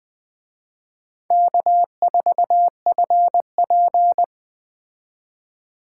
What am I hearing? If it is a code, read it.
Morse code, heard as a keyed tone, about 20 wpm, 700 Hz. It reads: K4FP